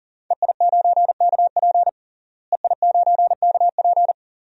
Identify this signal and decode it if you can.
Morse code, heard as a keyed tone, about 40 wpm, 690 Hz. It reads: EI9KP EI9KP
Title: DXCC-100 Call Signs - 5 Characters - Repeated 2x - 40wpm